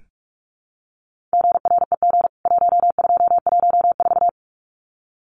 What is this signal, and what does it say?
Morse code, heard as a keyed tone, about 44 wpm, 700 Hz. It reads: GREG 1214
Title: CWT Exchange 45wpm